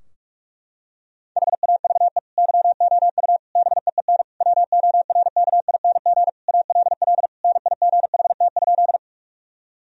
Morse code, heard as a keyed tone, about 42 wpm, 700 Hz. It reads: HAVE YOU BEEN WORKING ALL NIGHT?
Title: Sentences from Top 1000 Words - Review Part 4 - 45wpm